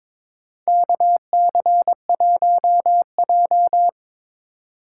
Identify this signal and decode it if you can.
Morse code, heard as a keyed tone, about 22 wpm, 700 Hz. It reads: KC1J
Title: USA Call Signs - 4 Characters - 2x Character Spacing - 22wpm